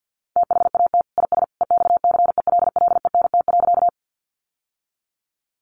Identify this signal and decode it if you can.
Morse code, heard as a keyed tone, about 50 wpm, 700 Hz. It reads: THAT IS EXCELLENT.
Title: Sentences from Top 1000 Words 50wpm